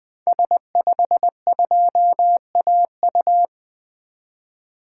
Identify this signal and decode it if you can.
Morse code, heard as a keyed tone, about 20 wpm, 700 Hz. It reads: S52AU
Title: 2023 SOTA Activator Call Signs - 20wpm